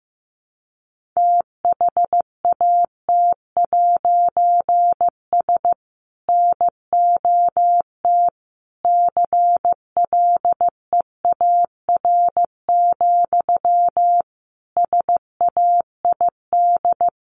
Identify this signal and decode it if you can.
Morse code, heard as a keyed tone, about 15 wpm, 700 Hz. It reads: THAT'S NOT CLEAR, SAID
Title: A Sound Of Thunder 15wpm